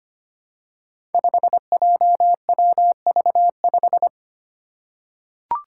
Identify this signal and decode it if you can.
Morse code, heard as a keyed tone, about 25 wpm, 700 Hz. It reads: 5JWV5
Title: Five Challenging Letter-Number Combinations 25wpm